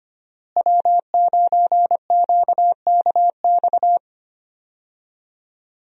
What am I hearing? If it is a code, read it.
Morse code, heard as a keyed tone, about 25 wpm, 700 Hz. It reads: W9QKX
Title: USA Call Signs - 5 Characters - Repeated 2x - 25wpm